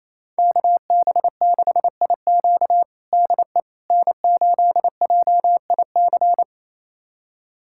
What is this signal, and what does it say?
Morse code, heard as a keyed tone, about 27 wpm, 700 Hz. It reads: KB6IQ DE N8JIC
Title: USA Callsign DE Callsign - 5 Characters - 28wpm